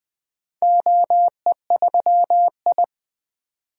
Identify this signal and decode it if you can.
Morse code, heard as a keyed tone, about 20 wpm, 700 Hz. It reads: OE3I
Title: DXCC-100 Call Signs - 4 Characters - 20wpm